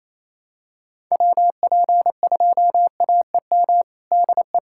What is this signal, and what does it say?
Morse code, heard as a keyed tone, about 28 wpm, 700 Hz. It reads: WP2AEM DE
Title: USA Callsign DE Callsign - 6 Characters - 28wpm